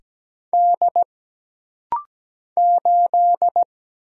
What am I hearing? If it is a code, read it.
Morse code, heard as a keyed tone, about 17 wpm, 700 Hz. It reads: D 8
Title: Single Letter-Number Alt - Rapid-Fire 17wpm